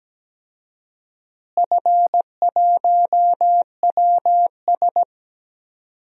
Morse code, heard as a keyed tone, about 17 wpm, 700 Hz. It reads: F1WS
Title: DXCC-100 Call Signs - 4 Characters - Repeated 2x - 17wpm